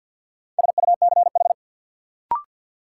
Morse code, heard as a keyed tone, about 48 wpm, 700 Hz. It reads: SUCH
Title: Sentences from Top 200 Words - New Word Drill - 50wpm